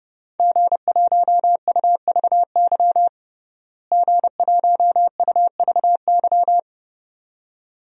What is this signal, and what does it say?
Morse code, heard as a keyed tone, about 32 wpm, 700 Hz. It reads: G1UVY G1UVY
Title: DXCC-100 Call Signs - 5 Characters - Repeated 2x - 30wpm